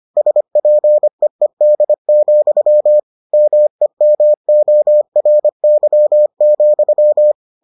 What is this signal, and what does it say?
Morse code, heard as a keyed tone, about 25 wpm, 600 Hz. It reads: SPEED, MEMORY,